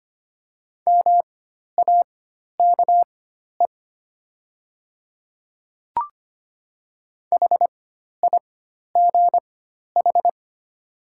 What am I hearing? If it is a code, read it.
Morse code, heard as a keyed tone, about 25 wpm, spaced as for 12 wpm, 700 Hz. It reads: MAKE HIGH
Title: 4x Character Spacing - Top 200 Words - 25wpm